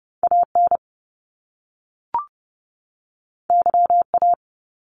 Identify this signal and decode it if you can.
Morse code, heard as a keyed tone, about 30 wpm, 700 Hz. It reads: AN YA
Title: Top Two Letter Combinations in Words 30wpm